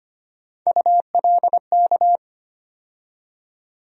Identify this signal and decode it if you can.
Morse code, heard as a keyed tone, about 25 wpm, 700 Hz. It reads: ULK
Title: USA Call Sign Postfixes - 3 Letters 25wpm